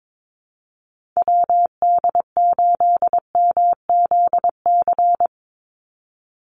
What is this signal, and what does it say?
Morse code, heard as a keyed tone, about 22 wpm, 700 Hz. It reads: WD8MZC
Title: USA Call Signs 22wpm